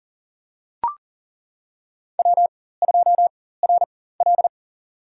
Morse code, heard as a keyed tone, about 40 wpm, spaced as for 19 wpm, 700 Hz. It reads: W2RL
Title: USA Call Signs - 4x Character Spacing - 40wpm